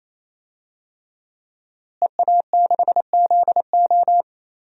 Morse code, heard as a keyed tone, about 28 wpm, 700 Hz. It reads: EA6ZO
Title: DXCC-100 Call Signs - 5 Characters - Repeated 3x - 28wpm